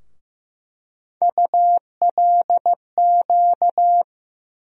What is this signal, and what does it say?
Morse code, heard as a keyed tone, about 15 wpm, 700 Hz. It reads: ULQ
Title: USA Call Sign Postfixes - 3 Letters 15wpm